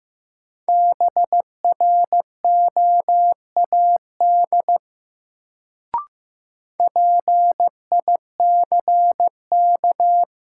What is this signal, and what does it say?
Morse code, heard as a keyed tone, about 15 wpm, 700 Hz. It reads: BROAD PICK
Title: Sentences from Top 1000 Words - New Word Drill - 15wpm